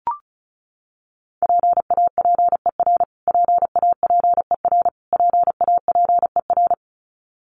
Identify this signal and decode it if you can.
Morse code, heard as a keyed tone, about 34 wpm, 700 Hz. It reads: PAPER PAPER PAPER
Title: Top 300 Words Repeated 35wpm